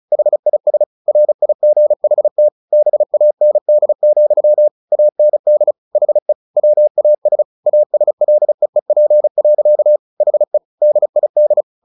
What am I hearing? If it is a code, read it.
Morse code, heard as a keyed tone, about 34 wpm, 600 Hz. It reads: HIS RIGHT BAND, AND HE WAS ASLEEP. HE DID
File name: the old man and the sea 2 35wpm